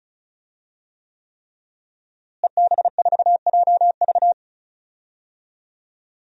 Morse code, heard as a keyed tone, about 34 wpm, 700 Hz. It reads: EB4JV
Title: DXCC-100 Call Signs - 5 Characters - Repeated 3x - 35wpm